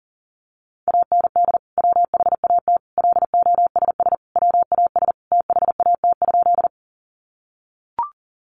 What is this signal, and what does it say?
Morse code, heard as a keyed tone, about 40 wpm, 700 Hz. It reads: AND WHAT LOSS WAS THAT?
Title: Sentences from Top 1400 Words 40wpm